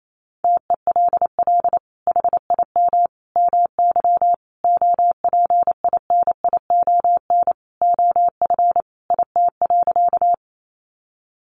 Morse code, heard as a keyed tone, about 28 wpm, 700 Hz. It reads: TELL HIM MY OPINION OF IT.
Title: Sentences from Top 1200 Words 28wpm